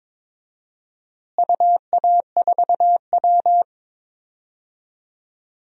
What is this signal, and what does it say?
Morse code, heard as a keyed tone, about 22 wpm, 700 Hz. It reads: UA4W
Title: DXCC-100 Call Signs - 4 Characters - Repeated 2x - 22wpm